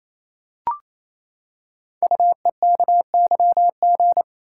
Morse code, heard as a keyed tone, about 27 wpm, 700 Hz. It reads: UEKYG
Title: Code Groups - 5 Letters - 28wpm